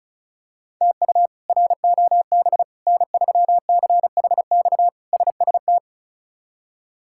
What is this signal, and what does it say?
Morse code, heard as a keyed tone, about 36 wpm, 700 Hz. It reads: TU ROB N3CHX SST
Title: K1USN SST QSO Practice - 35wpm